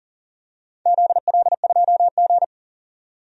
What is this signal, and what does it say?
Morse code, heard as a keyed tone, about 40 wpm, 700 Hz. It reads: ZL2G
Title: DXCC-100 Call Signs - 4 Characters - 40wpm